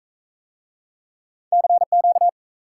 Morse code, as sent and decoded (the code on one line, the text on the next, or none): -.-. --.-
CQ